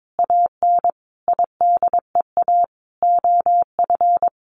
Morse code, heard as a keyed tone, about 22 wpm, 700 Hz. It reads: AN IDEA OF
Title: Sentences from Top 1500 Words 22wpm